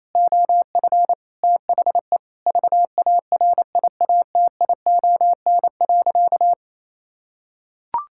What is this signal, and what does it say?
Morse code, heard as a keyed tone, about 28 wpm, 700 Hz. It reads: OF THE VARIATION.
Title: Sentences from Top 1600 Words 28wpm